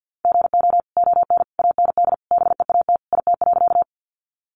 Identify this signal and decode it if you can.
Morse code, heard as a keyed tone, about 50 wpm, 700 Hz. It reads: GO ON AND BEAT IT.
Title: Sentences from Top 1300 Words 50wpm